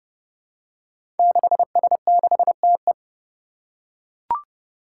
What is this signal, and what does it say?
Morse code, heard as a keyed tone, about 30 wpm, 700 Hz. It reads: 6S6TE
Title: Five Challenging Letter-Number Combinations 30wpm